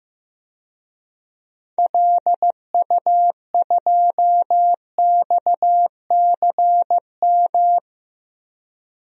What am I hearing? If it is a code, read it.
Morse code, heard as a keyed tone, about 15 wpm, 700 Hz. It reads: LU2XCM
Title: DXCC-100 Call Signs - 6 Characters - 15wpm